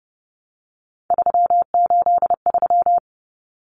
Morse code, heard as a keyed tone, about 30 wpm, 700 Hz. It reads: 383
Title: Numbers - 3-Digit 30wpm